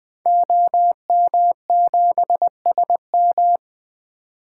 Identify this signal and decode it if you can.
Morse code, heard as a keyed tone, about 20 wpm, 700 Hz. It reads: OM7SM
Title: 2023 SOTA Activator Call Signs - 20wpm